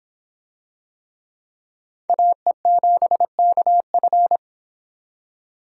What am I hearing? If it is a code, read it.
Morse code, heard as a keyed tone, about 25 wpm, 700 Hz. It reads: AE7KF